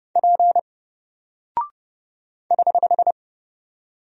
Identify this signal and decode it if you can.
Morse code, heard as a keyed tone, about 32 wpm, 700 Hz. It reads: P <HH>
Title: Single Character (Letter:Number:Punctuation:Prosign) 30wpm